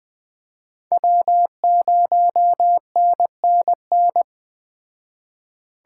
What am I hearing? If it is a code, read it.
Morse code, heard as a keyed tone, about 21 wpm, 700 Hz. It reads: W0NNN
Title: Speed Racing - USA 1x3 Call Signs 17wpm